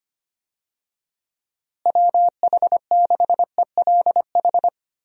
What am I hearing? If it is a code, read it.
Morse code, heard as a keyed tone, about 25 wpm, 700 Hz. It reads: WH6ELH
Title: USA Call Signs - 6 Characters - Repeated 2x - 25wpm